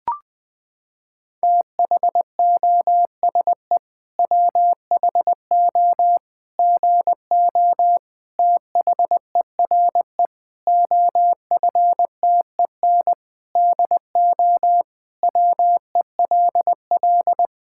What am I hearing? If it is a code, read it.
Morse code, heard as a keyed tone, about 20 wpm, 700 Hz. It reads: THOSE WHO GO THERE OFTEN DO WELL
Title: Sentences from Top 1000 Words - Review Part 5 - 20wpm